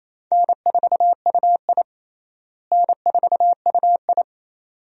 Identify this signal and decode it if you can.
Morse code, heard as a keyed tone, about 28 wpm, 700 Hz. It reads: N4UI N4UI
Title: USA Call Signs - 4 Characters - Repeated 2x - 28wpm